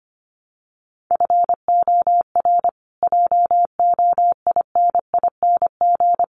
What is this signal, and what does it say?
Morse code, heard as a keyed tone, about 25 wpm, 700 Hz. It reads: FOR JOINING
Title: Sets of 5 Words Encore 25wpm